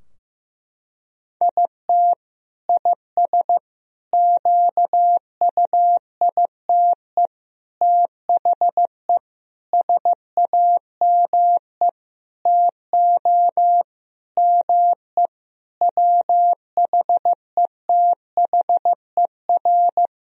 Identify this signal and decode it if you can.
Morse code, heard as a keyed tone, about 15 wpm, 700 Hz. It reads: IT IS QUITE THE SAME TO ME WHETHER
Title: Sentences from Top 1000 Words - Review Part 3 - 15wpm